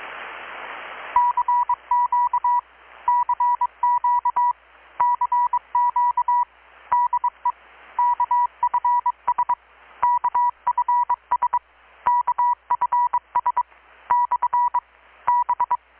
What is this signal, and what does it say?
Morse code, heard as a keyed tone, about 22 wpm, 970 Hz, with radio noise. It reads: CQ CQ CQ DE KFS KFS KFS / B
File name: blog / swl / 6-abril-2019-12695-USB-KPS